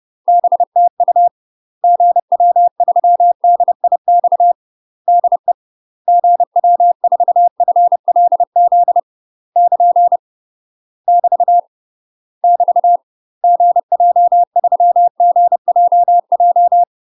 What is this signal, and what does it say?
Morse code, heard as a keyed tone, about 30 wpm, 700 Hz. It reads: BTU GW3DIX DE GW4FLZ <KN> <BT> <BT> GJ3GJJ
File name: UK MASTER QSOs TRAINING / 30WPM / UK MASTER QSOs 0018